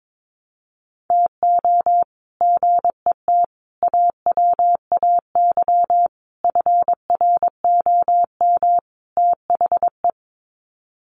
Morse code, heard as a keyed tone, about 22 wpm, 700 Hz. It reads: TO GET AWAY FROM THE
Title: Sets of 5 Words Encore2 22wpm